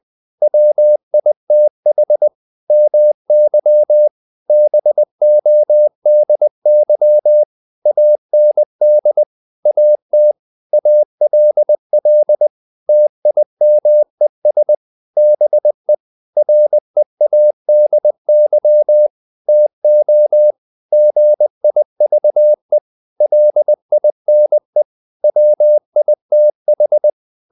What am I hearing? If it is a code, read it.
Morse code, heard as a keyed tone, about 20 wpm, 600 Hz. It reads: WITH MY BODY AND AT ALL TIMES BE READY TO GIVE LINE WITH
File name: the old man and the sea part 2 20wpm